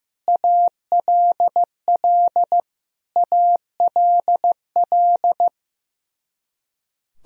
Morse code, heard as a keyed tone, about 15 wpm, 700 Hz. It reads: ALL ALL
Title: Top 100 Words Repeated 15wpm